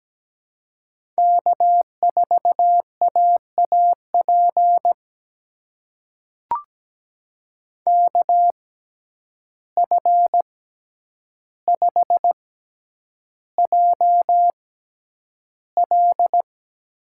Morse code, heard as a keyed tone, about 17 wpm, spaced as for 9 wpm, 700 Hz. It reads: K4AAP KF5JL
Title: USA Call Signs - 5 Characters - 6x Character Spacing - 17wpm